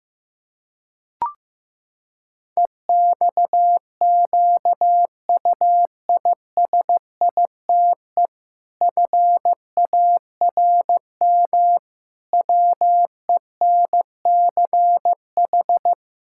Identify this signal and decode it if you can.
Morse code, heard as a keyed tone, about 16 wpm, 700 Hz. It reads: EXQUISITE FARM WENCH
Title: Pangram Challenge v2 - 15wpm